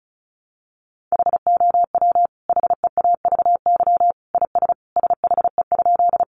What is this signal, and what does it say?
Morse code, heard as a keyed tone, about 34 wpm, 700 Hz. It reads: HOW HEAVY IS SHE?
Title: Sentences from Top 1100 Words 35wpm